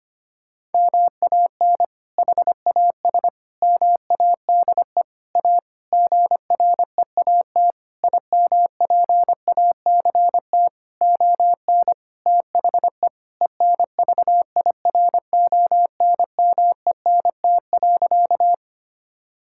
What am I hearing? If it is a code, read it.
Morse code, heard as a keyed tone, about 25 wpm, 700 Hz. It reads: MAN HAS MADE A GREAT IMPACT ON THE ENVIRONMENT.
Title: Sentences from Top 1000 Words - Review Part 6 - 25wpm